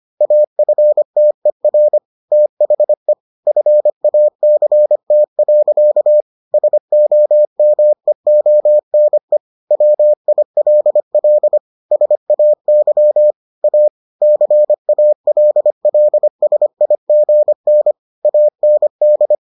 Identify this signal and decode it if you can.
Morse code, heard as a keyed tone, about 25 wpm, 600 Hz. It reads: AFTER THE FACT. SOMEONE WILL SAY A CALLSIGN AND